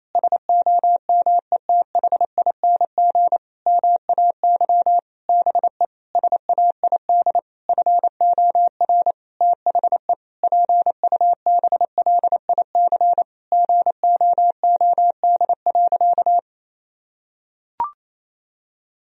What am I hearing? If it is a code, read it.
Morse code, heard as a keyed tone, about 28 wpm, 700 Hz. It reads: SOMETHING MAY BE SAID FOR THE PUBLIC GOOD.
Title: Sentences from Top 1000 Words - Review Part 3 - 28wpm